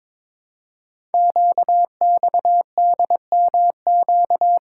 Morse code, heard as a keyed tone, about 22 wpm, 700 Hz. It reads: QXDMQ
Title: Code Groups - 5 Letters - 22wpm